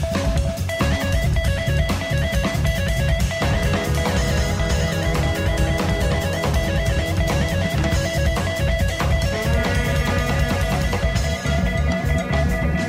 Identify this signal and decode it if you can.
Two-tone teleprinter signal, about 80 Hz shift.